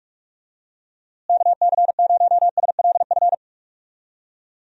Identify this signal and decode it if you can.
Morse code, heard as a keyed tone, about 41 wpm, 700 Hz. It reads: KC0SLF
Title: USA Call Signs - 6 Characters - Repeated 3x - 45wpm